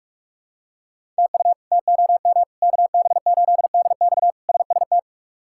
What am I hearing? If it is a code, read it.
Morse code, heard as a keyed tone, about 46 wpm, 700 Hz. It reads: TU TOM KB8DX SST
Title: K1USN SST QSO Practice - 45wpm